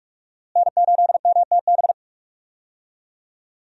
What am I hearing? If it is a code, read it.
Morse code, heard as a keyed tone, about 46 wpm, 700 Hz. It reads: N8MTB
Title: USA Call Signs - 5 Characters - Repeated 3x - 45wpm